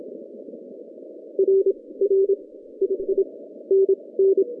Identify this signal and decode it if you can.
Morse code, heard as a keyed tone, about 26 wpm, 380 Hz. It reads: R R II N N
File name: 2015 cq ww cw / 2015 cqwww k5tr - 0-27